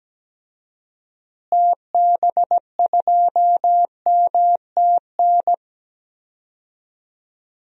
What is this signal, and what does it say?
Morse code, heard as a keyed tone, about 17 wpm, 700 Hz. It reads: TB2MTN
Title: DXCC-100 Call Signs - 6 Characters - Repeated 2x - 17wpm